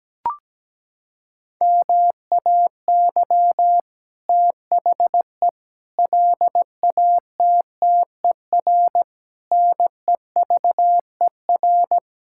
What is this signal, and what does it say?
Morse code, heard as a keyed tone, about 17 wpm, 700 Hz. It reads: MAY THE LATTER NEVER
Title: Sentences from Top 1600 Words 17wpm